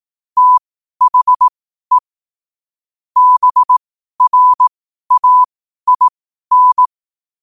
Morse code, heard as a keyed tone, about 18 wpm, spaced as for 12 wpm, 1000 Hz. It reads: THE BRAIN
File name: qotdfiles / 12wpm